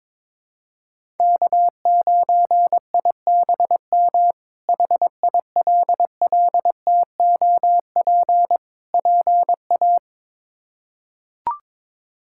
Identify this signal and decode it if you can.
Morse code, heard as a keyed tone, about 22 wpm, 700 Hz. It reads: K9IBM HILLTOP PA
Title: K1USN SST QSO Practice - 22wpm